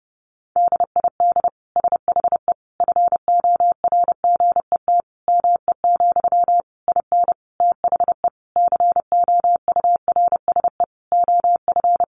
Morse code, heard as a keyed tone, about 28 wpm, 700 Hz. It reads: DID SHE FORGET ME, IN THE COURSE OF
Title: Sentences from Top 1200 Words 30wpm